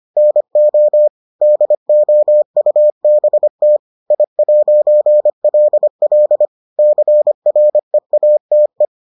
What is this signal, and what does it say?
Morse code, heard as a keyed tone, about 25 wpm, 600 Hz. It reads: NO DOUBT I'LL CREATE